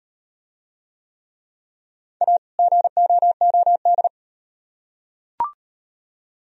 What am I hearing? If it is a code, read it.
Morse code, heard as a keyed tone, about 38 wpm, 700 Hz. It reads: A GOOD